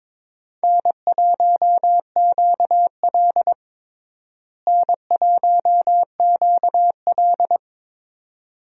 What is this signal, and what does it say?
Morse code, heard as a keyed tone, about 23 wpm, 700 Hz. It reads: N1QL N1QL
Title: USA Call Signs - 4 Characters - Repeated 2x - 22wpm